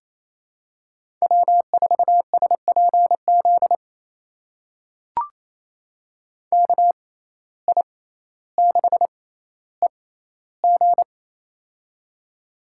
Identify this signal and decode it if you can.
Morse code, heard as a keyed tone, about 28 wpm, spaced as for 14 wpm, 700 Hz. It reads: W4SPZ KI6EG
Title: USA Call Signs - 5 Characters - 6x Character Spacing - 28wpm